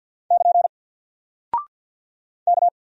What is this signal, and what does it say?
Morse code, heard as a keyed tone, about 48 wpm, 700 Hz. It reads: <KN> K
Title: Single Character (Letter:Number:Punctuation:Prosign) Alt 50wpm